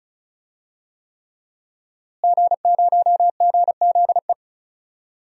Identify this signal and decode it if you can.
Morse code, heard as a keyed tone, about 36 wpm, 700 Hz. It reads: G0GZE
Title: DXCC-100 Call Signs - 5 Characters - Repeated 3x - 35wpm